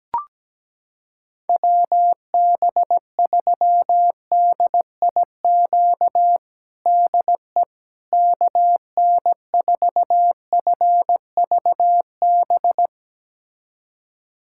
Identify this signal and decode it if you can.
Morse code, heard as a keyed tone, about 17 wpm, 700 Hz. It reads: WB3DIQ DE KN4FVB
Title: USA Callsign DE Callsign - 6 Characters - 17wpm